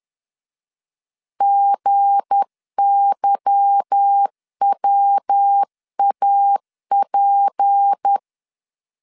Morse code, heard as a keyed tone, about 10 wpm, 800 Hz. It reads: GYWAP
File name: LEZIONI / 8 YJ45 / 8 Y 10wpm 2615